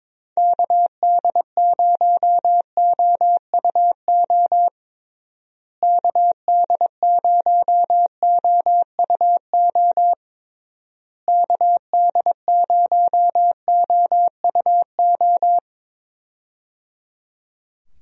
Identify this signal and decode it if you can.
Morse code, heard as a keyed tone, about 22 wpm, 700 Hz. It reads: KD0OUO KD0OUO KD0OUO
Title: USA Call Signs - 6 Characters - Repeated 3x - 22wpm